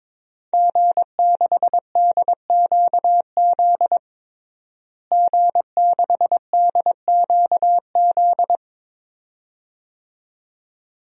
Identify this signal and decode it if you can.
Morse code, heard as a keyed tone, about 22 wpm, 700 Hz. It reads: G6DQZ G6DQZ
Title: DXCC-100 Call Signs - 5 Characters - Repeated 2x - 22wpm